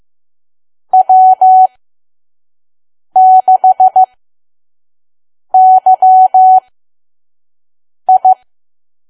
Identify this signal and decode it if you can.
Morse code, heard as a keyed tone, about 15 wpm, spaced as for 5 wpm, 750 Hz. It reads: W6YI
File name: Archive / 5 WPM files / PM